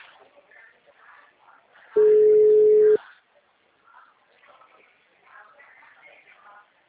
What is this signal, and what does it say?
Ringback tone (Europe, 425 Hz).